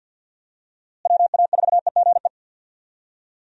Morse code, heard as a keyed tone, about 50 wpm, 700 Hz. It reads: WA4EGE
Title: USA Call Signs - 6 Characters - Repeated 3x - 50wpm